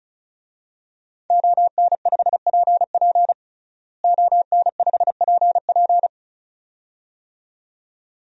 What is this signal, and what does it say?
Morse code, heard as a keyed tone, about 34 wpm, 700 Hz. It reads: ON5PP ON5PP
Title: DXCC-100 Call Signs - 5 Characters - Repeated 2x - 35wpm